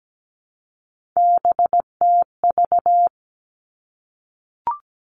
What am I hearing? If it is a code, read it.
Morse code, heard as a keyed tone, about 17 wpm, 700 Hz. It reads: BTV
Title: Three Challenging Letter-Number Combinations 17wpm